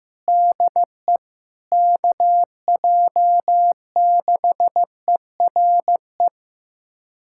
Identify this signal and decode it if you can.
Morse code, heard as a keyed tone, about 15 wpm, 700 Hz. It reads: DE KJ6ERE
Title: USA Callsign DE Callsign - 6 Characters - 15wpm